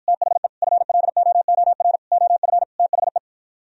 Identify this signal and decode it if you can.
Morse code, heard as a keyed tone, about 52 wpm, 700 Hz. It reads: THE FLOOR OF THE